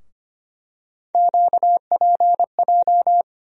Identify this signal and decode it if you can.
Morse code, heard as a keyed tone, about 25 wpm, 700 Hz. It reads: QPJ
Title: USA Call Sign Postfixes - 3 Letters 25wpm